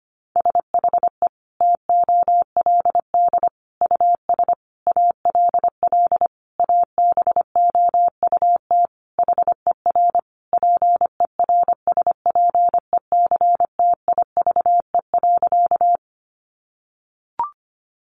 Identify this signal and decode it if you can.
Morse code, heard as a keyed tone, about 25 wpm, 700 Hz. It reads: SHE TOLD US ALL ABOUT HER PERSPECTIVE.
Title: Sentences from Top 1200 Words 25wpm